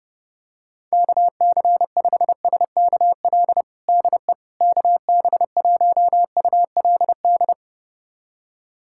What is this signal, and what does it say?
Morse code, heard as a keyed tone, about 32 wpm, 700 Hz. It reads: KC5SKL DE KB1ULD
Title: USA Callsign DE Callsign - 6 Characters - 30wpm